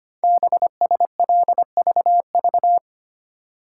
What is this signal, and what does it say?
Morse code, heard as a keyed tone, about 25 wpm, 700 Hz. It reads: BSLVV
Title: Code Groups - 5 Letters - 25wpm